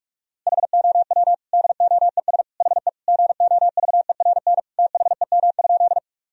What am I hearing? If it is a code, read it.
Morse code, heard as a keyed tone, about 46 wpm, 700 Hz. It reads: HOW DOES HE GOVERN THEM?
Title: Sentences from Top 1000 Words - Review Part 2 - 45wpm